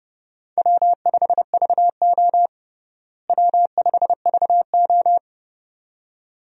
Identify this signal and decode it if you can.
Morse code, heard as a keyed tone, about 30 wpm, 700 Hz. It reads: W5VO W5VO
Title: USA Call Signs - 4 Characters - Repeated 2x - 30wpm